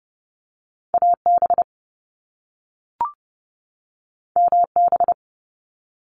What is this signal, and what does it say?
Morse code, heard as a keyed tone, about 30 wpm, 700 Hz. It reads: AB MB
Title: Canadian Provinces and Territories 30wpm